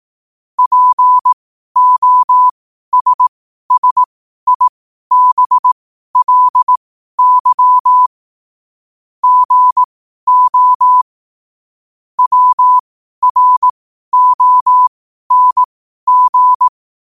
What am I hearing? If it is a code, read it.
Morse code, heard as a keyed tone, about 18 wpm, spaced as for 12 wpm, 1000 Hz. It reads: POSSIBLY GO WRONG
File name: qotdfiles / 12wpm